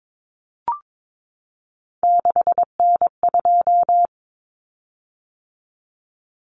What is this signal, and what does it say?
Morse code, heard as a keyed tone, about 22 wpm, 700 Hz. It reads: 6N2
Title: Common Cut Numbers - 3-Digit 22wpm